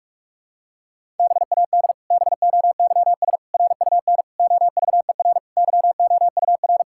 Morse code, heard as a keyed tone, about 41 wpm, 700 Hz. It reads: BAD BOYS RUN OVER YOUR
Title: Pangram Challenge v2 - 45wpm